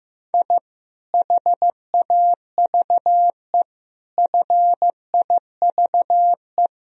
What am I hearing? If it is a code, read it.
Morse code, heard as a keyed tone, about 15 wpm, 700 Hz. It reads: I HAVE FIVE
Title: Sentences from Top 1000 Words - Review Part 2 - 15wpm